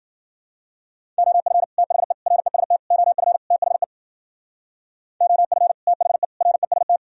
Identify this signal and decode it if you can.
Morse code, heard as a keyed tone, about 59 wpm, 700 Hz. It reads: OF THE REST OF THE OF THE REST